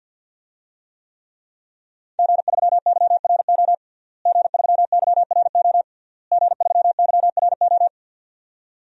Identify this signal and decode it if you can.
Morse code, heard as a keyed tone, about 52 wpm, 700 Hz. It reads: G3YRO G3YRO G3YRO
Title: DXCC-100 Call Signs - 5 Characters - Repeated 3x - 50wpm